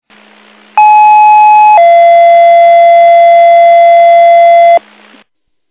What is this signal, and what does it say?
Two-tone page, heard as a one-second tone, then a three-second tone, about 850 then 680 Hz, with noise. It is heard from 0.8 to 4.8 s.
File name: Old Forge